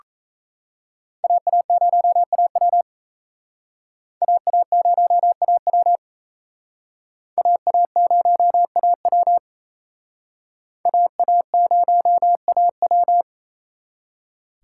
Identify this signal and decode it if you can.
Morse code, heard as a keyed tone, about 34 wpm, 700 Hz. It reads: AA0AW AA0AW AA0AW AA0AW
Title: Speed Racing - USA 2x2 Call Signs 28wpm